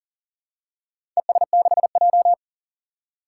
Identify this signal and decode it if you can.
Morse code, heard as a keyed tone, about 39 wpm, 700 Hz. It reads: ES6J
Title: Four Challenging Letter-Number Combinations 40wpm